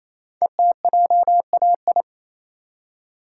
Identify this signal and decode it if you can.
Morse code, heard as a keyed tone, about 29 wpm, 700 Hz. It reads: ETJAI
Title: Code Groups - 5 Letters - 28wpm